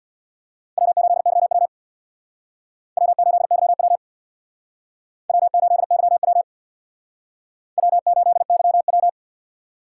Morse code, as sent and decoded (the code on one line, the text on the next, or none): .-- ---.. -.-- .-- / .-- ---.. -.-- .-- / .-- ---.. -.-- .-- / .-- ---.. -.-- .--
W8YW W8YW W8YW W8YW